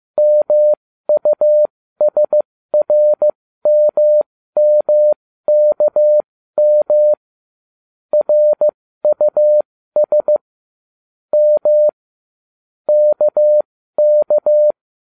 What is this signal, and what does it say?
Morse code, heard as a keyed tone, about 15 wpm, spaced as for 13 wpm, 600 Hz. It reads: MUSRMMKM RUS M KK